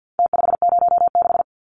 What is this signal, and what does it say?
Morse code, heard as a keyed tone, about 49 wpm, 700 Hz. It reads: T506
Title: Numbers - Contest Style - 4-Digit 50wpm